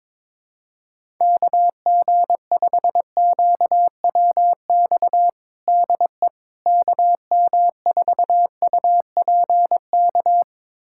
Morse code, heard as a keyed tone, about 22 wpm, 700 Hz. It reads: KG5QWX DE KM4UPK
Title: USA Callsign DE Callsign - 6 Characters - 22wpm